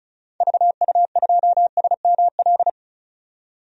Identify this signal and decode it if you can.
Morse code, heard as a keyed tone, about 34 wpm, 700 Hz. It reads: VU2SML
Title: DXCC-100 Call Signs - 6 Characters - 35wpm